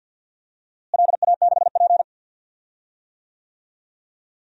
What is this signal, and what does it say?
Morse code, heard as a keyed tone, about 53 wpm, 700 Hz. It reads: LA6P
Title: DXCC-100 Call Signs - 4 Characters - Repeated 2x - 50wpm